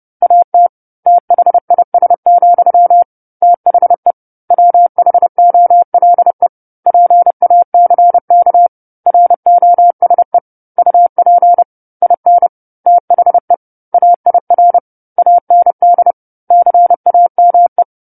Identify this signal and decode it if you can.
Morse code, heard as a keyed tone, about 30 wpm, 700 Hz. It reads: AT THIS, THE WHOLE PACK ROSE UP IN THE AIR AND CAME
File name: 10-Chapter-10 30 WPM